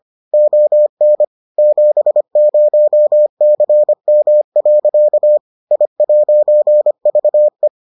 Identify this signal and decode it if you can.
Morse code, heard as a keyed tone, about 25 wpm, 600 Hz. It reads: ON 70CM. I'VE